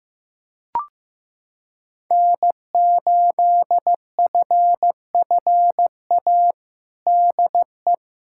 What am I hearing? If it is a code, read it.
Morse code, heard as a keyed tone, about 15 wpm, 700 Hz. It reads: N8FFA DE
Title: USA Callsign DE Callsign 15wpm